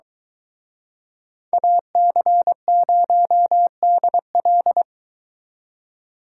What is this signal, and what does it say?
Morse code, heard as a keyed tone, about 23 wpm, 700 Hz. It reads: AC0DL